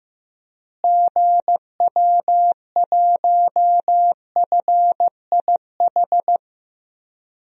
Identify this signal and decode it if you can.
Morse code, heard as a keyed tone, about 15 wpm, 700 Hz. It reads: GW1FIH
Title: DXCC-100 Call Signs - 6 Characters - Repeated 3x - 15wpm